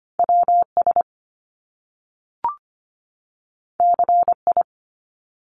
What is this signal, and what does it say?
Morse code, heard as a keyed tone, about 25 wpm, 700 Hz. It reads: WS CI
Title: Top Two Letter Combinations in Words 25wpm